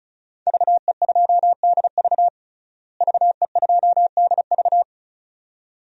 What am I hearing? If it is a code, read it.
Morse code, heard as a keyed tone, about 36 wpm, 700 Hz. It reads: VE2DV VE2DV
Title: DXCC-100 Call Signs - 5 Characters - Repeated 2x - 35wpm